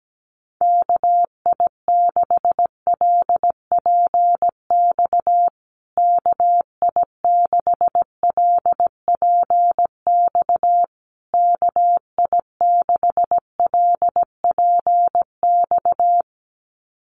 Morse code, heard as a keyed tone, about 17 wpm, 700 Hz. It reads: KI6LPX KI6LPX KI6LPX
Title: USA Call Signs Repeated 3x 17wpm